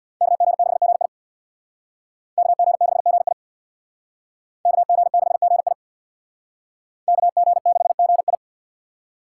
Morse code, heard as a keyed tone, about 60 wpm, 700 Hz. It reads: KG6GI KG6GI KG6GI KG6GI
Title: Speed Racing - USA 2x2 Call Signs 50wpm